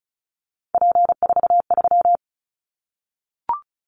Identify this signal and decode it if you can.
Morse code, heard as a keyed tone, about 34 wpm, 700 Hz. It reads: P43
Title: Callsign Prefixes for Top 100 DXCC - 35wpm